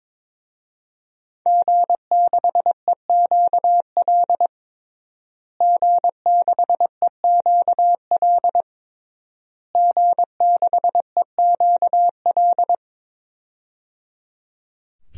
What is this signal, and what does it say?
Morse code, heard as a keyed tone, about 22 wpm, 700 Hz. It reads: G6EQL G6EQL G6EQL
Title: DXCC-100 Call Signs - 5 Characters - Repeated 3x - 22wpm